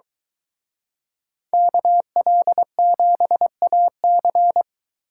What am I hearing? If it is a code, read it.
Morse code, heard as a keyed tone, about 23 wpm, 700 Hz. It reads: KL7AC